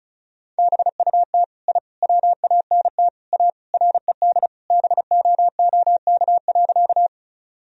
Morse code, heard as a keyed tone, about 36 wpm, 700 Hz. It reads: BUT I WANT A RED BOOK.
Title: Sentences from Top 1000 Words - Review Part 2 - 35wpm